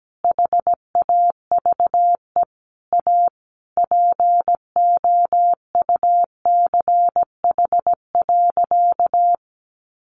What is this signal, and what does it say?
Morse code, heard as a keyed tone, about 17 wpm, 700 Hz. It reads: HAVE A POUCH.
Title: Easy Phrases 17wpm